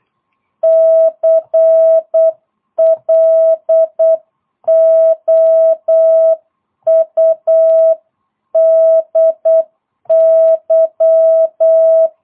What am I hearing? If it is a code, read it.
Morse code, heard as a keyed tone, about 8 wpm, 650 Hz. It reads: CLOUDY
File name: jh8cbh / 2022cw